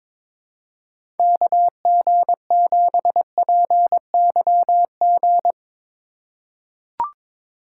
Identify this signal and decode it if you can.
Morse code, heard as a keyed tone, about 21 wpm, 700 Hz. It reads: KG7PYG
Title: USA Call Signs - 3x Character Spacing - 22wpm